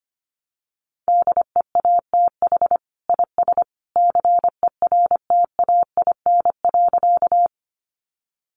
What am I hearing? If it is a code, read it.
Morse code, heard as a keyed tone, about 25 wpm, 700 Hz. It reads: DEATH IS CERTAIN.
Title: Sentences from Top 1100 Words 25wpm